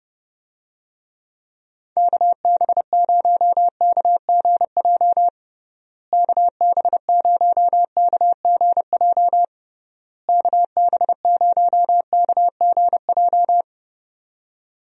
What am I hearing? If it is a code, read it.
Morse code, heard as a keyed tone, about 30 wpm, 700 Hz. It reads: KB0KGJ KB0KGJ KB0KGJ
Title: USA Call Signs - 6 Characters - Repeated 3x - 30wpm